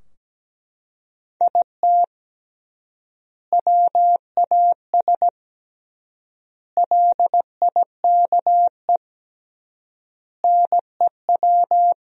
Morse code, heard as a keyed tone, about 17 wpm, 700 Hz. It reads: IT WAS LIKE NEW
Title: 3x Word-Spacing - Sentences from Top 300 Words - 17wpm